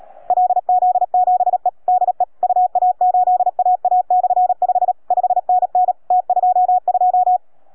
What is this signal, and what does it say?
Morse code, heard as a keyed tone, about 30 wpm, 720 Hz. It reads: LZ7E DE UA8AA/5 5NN T22